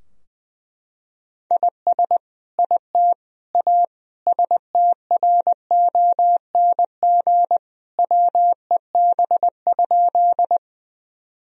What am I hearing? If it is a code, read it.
Morse code, heard as a keyed tone, about 21 wpm, 700 Hz. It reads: IS IT A STRONG WEB?
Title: Sentences from Top 1000 Words - Review Part 4 - 20wpm